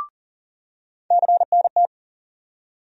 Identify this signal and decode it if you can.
Morse code, heard as a keyed tone, about 40 wpm, 700 Hz. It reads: CNT
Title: USA Call Sign Postfixes - 3 Letters 40wpm